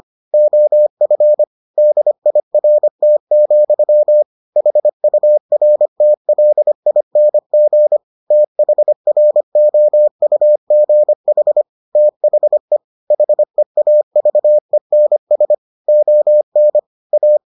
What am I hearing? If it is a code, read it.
Morse code, heard as a keyed tone, about 25 wpm, 600 Hz. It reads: OF DIRT, HURTLING THROUGH THE HEAVENS ON A